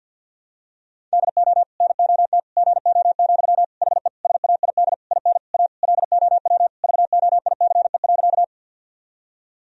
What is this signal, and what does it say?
Morse code, heard as a keyed tone, about 46 wpm, 700 Hz. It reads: DO NOT GO, HE SAID IN A LOW VOICE.
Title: Sentences from Top 1000 Words - Review Part 1 - 50wpm